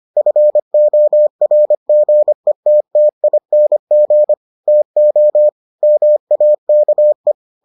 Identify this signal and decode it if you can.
Morse code, heard as a keyed tone, about 25 wpm, 600 Hz. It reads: FORGETTING TO MAKE